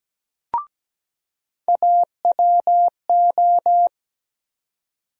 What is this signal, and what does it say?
Morse code, heard as a keyed tone, about 17 wpm, 700 Hz. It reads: AWO
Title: USA Call Sign Postfixes - 3 Letters 17wpm